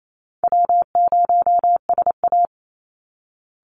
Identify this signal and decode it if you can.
Morse code, heard as a keyed tone, about 28 wpm, 700 Hz. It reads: W0SA
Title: Call Signs Repeated 3x - 28wpm